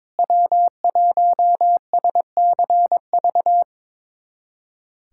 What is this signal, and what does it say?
Morse code, heard as a keyed tone, about 22 wpm, 700 Hz. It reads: W1SCV
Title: USA Callsign DE Callsign - 5 Characters - 22wpm